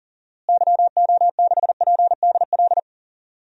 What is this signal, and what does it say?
Morse code, heard as a keyed tone, about 40 wpm, 700 Hz. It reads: YO6PDL
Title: DXCC-100 Call Signs - 6 Characters - Repeated 2x - 40wpm